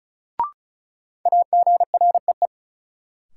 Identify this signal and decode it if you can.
Morse code, heard as a keyed tone, about 34 wpm, 700 Hz. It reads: AGREE
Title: Sentences from Top 1000 Words - Review Part 6 - New Word Drill - 35wpm